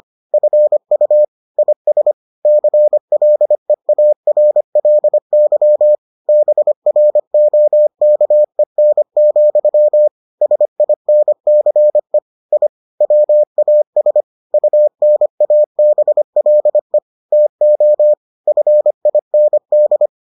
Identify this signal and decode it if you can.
Morse code, heard as a keyed tone, about 25 wpm, 600 Hz. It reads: FU IS CLEARLY BROKEN, SINCE I WAS UNABLE TO FIND